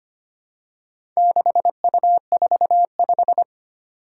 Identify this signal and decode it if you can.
Morse code, heard as a keyed tone, about 25 wpm, 700 Hz. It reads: 6U45
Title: Four Challenging Letter-Number Combinations 25wpm